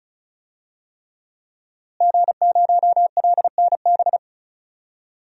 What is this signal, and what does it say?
Morse code, heard as a keyed tone, about 34 wpm, 700 Hz. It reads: G0LNB
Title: DXCC-100 Call Signs - 5 Characters - Repeated 3x - 35wpm